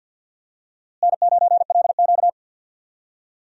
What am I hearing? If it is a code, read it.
Morse code, heard as a keyed tone, about 47 wpm, 700 Hz. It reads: N9LQ
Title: USA Call Signs - 4 Characters - Repeated 3x - 50wpm